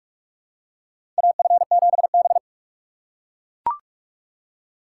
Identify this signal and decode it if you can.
Morse code, heard as a keyed tone, about 41 wpm, 700 Hz. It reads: AF7B
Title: USA Call Signs - 4x Character Spacing - 45wpm